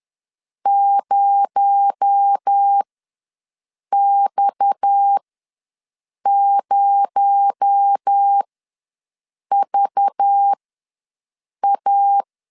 Morse code, heard as a keyed tone, about 10 wpm, spaced as for 6 wpm, 800 Hz. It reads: 0X0VA
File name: LEZIONI / 6 PKX0 / 6 PKX0 ws 10wpm 4220